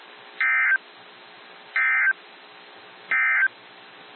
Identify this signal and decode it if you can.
Emergency Alert System (SAME) tones, heard as harsh buzzing bursts. It is heard from 0.3 to 3.6 s.
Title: NOAA Weather Radio - WXJ73